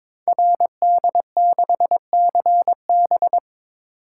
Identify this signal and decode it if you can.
Morse code, heard as a keyed tone, about 22 wpm, 700 Hz. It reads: RD6CB
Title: Code Groups - 5 Letter:Numbers - 22wpm